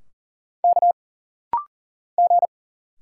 Morse code, heard as a keyed tone, about 40 wpm, 700 Hz. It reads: K G
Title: Single Letters - Rapid-Fire 40wpm